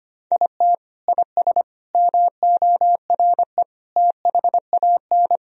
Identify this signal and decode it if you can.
Morse code, heard as a keyed tone, about 25 wpm, 700 Hz. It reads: IT IS MORE THAN
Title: Sentences from Top 1000 Words - Review Part 4 - 25wpm